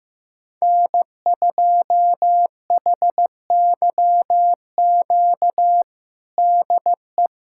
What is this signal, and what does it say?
Morse code, heard as a keyed tone, about 15 wpm, 700 Hz. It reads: N2HYQ DE
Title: USA Callsign DE Callsign - 5 Characters - 15wpm